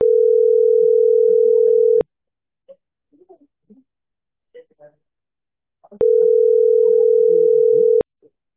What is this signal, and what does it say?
Ringback tone (North America), 440 + 480 Hz.